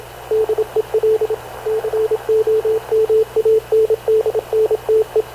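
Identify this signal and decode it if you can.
Morse code, heard as a keyed tone, about 28 wpm, 440 Hz, with radio noise. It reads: DEL COMANDNTE
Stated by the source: Havana Radio CLA press (PX) broadcast, recorded by Radio Officer Paul Dumesnil aboard mv CNAV QUEST/CZDO.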